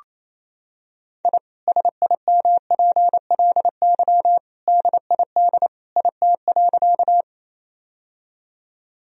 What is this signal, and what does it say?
Morse code, heard as a keyed tone, about 28 wpm, 700 Hz. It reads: I SIMPLY DID IT.
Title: Sentences from Top 1000 Words - Review Part 4 - 28wpm